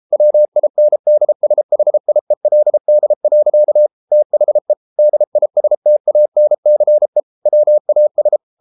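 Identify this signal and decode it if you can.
Morse code, heard as a keyed tone, about 33 wpm, 600 Hz. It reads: WINDSHIELD. THE DISTANCE WAS
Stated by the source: Morse code practice at 32 wpm from CW med Gnister, reading Ken Follett's novel Never in English.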